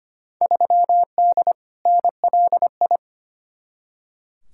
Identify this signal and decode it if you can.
Morse code, heard as a keyed tone, about 25 wpm, 700 Hz. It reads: 3D NLI
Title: Field Day Exchanges Encore - 25wpm